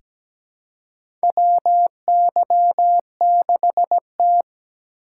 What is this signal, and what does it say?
Morse code, heard as a keyed tone, about 17 wpm, 700 Hz. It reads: WY6T
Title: USA Call Signs - 4 Characters - 4x Character Spacing - 17wpm